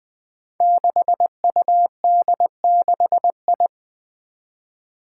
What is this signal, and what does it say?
Morse code, heard as a keyed tone, about 20 wpm, 700 Hz. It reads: 6UD6I
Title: Five Challenging Letter-Number Combinations 20wpm